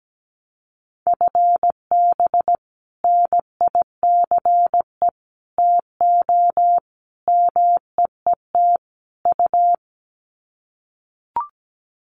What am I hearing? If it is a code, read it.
Morse code, heard as a keyed tone, about 17 wpm, 700 Hz. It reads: FB NICE TO MEET U
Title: Common QSO Phrases 17wpm